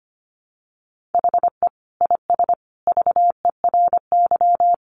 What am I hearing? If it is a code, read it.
Morse code, heard as a keyed tone, about 25 wpm, 700 Hz. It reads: HE IS VERY
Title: Sentences from Top 1400 Words 25wpm